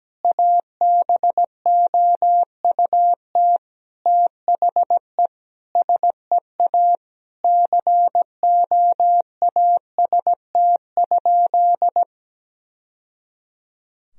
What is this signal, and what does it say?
Morse code, heard as a keyed tone, about 17 wpm, 700 Hz. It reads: ABOUT THE SEA COAST?
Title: Sentences from Top 1000 Words - Review Part 4 - 17wpm